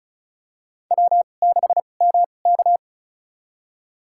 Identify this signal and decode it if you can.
Morse code, heard as a keyed tone, about 34 wpm, spaced as for 25 wpm, 700 Hz. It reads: W6MK
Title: USA Call Signs - 4 Characters - 2x Character Spacing - 35wpm